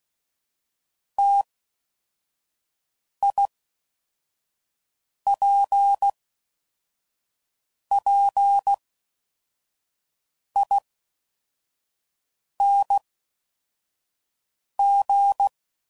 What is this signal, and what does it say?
Morse code, heard as a keyed tone, about 16 wpm, 800 Hz. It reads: T I P P I N G
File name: Mixed / Variable Sp / T AOEIs